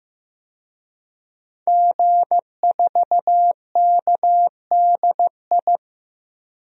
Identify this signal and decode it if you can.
Morse code, heard as a keyed tone, about 16 wpm, 700 Hz. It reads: G4KDI
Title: DXCC-100 Call Signs - 5 Characters - Repeated 3x - 15wpm